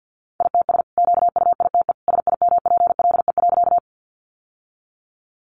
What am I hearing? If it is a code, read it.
Morse code, heard as a keyed tone, about 50 wpm, 700 Hz. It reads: ITS QUITE SIMPLE.
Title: Sentences from Top 700 Words 50wpm